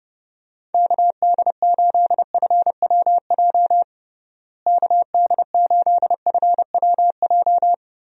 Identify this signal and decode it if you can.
Morse code, heard as a keyed tone, about 30 wpm, 700 Hz. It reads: KD8FWJ KD8FWJ
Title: USA Call Signs - 6 Characters - Repeated 3x - 30wpm